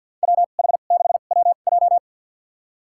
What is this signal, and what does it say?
Morse code, heard as a keyed tone, about 52 wpm, spaced as for 36 wpm, 700 Hz. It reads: WH6WJ
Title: USA Call Signs - 5 Characters - 2x Character Spacing - 50wpm